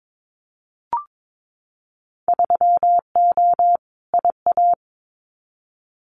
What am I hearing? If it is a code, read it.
Morse code, heard as a keyed tone, about 22 wpm, 700 Hz. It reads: 3O IA
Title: Winter Field Day Exchanges 22wpm